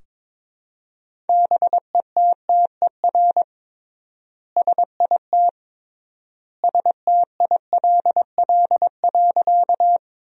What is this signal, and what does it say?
Morse code, heard as a keyed tone, about 22 wpm, 700 Hz. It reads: BETTER SIT STILL.
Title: 3x Word-Spacing - Sentences from Top 500 Words - 22wpm